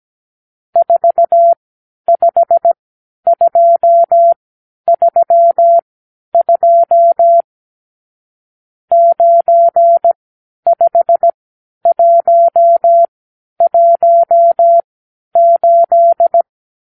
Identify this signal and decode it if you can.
Morse code, heard as a keyed tone, about 17 wpm, spaced as for 10 wpm, 680 Hz. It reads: 45232 95118
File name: arrl / Review (1-0)